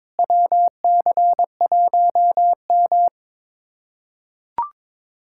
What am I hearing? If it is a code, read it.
Morse code, heard as a keyed tone, about 22 wpm, 700 Hz. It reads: WC1M
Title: USA Callsign DE Callsign - 4 Characters - 22wpm